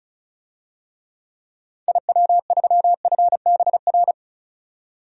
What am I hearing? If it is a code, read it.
Morse code, heard as a keyed tone, about 34 wpm, 700 Hz. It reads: IW3FBR
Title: DXCC-100 Call Signs - 6 Characters - 35wpm